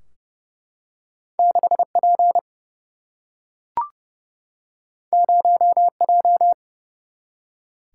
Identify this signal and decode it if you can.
Morse code, heard as a keyed tone, about 30 wpm, 700 Hz. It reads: 6 P 0 J
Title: Single-Letter Single-Number 30wpm